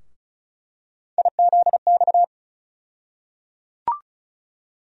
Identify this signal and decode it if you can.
Morse code, heard as a keyed tone, about 36 wpm, 700 Hz. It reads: IZX
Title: USA Call Sign Postfixes - 3 Letters 35wpm